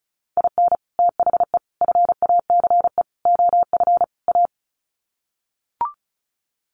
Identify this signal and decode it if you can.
Morse code, heard as a keyed tone, about 34 wpm, 700 Hz. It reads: IN THE FACE OF A
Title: Sets of 5 Words Encore 35wpm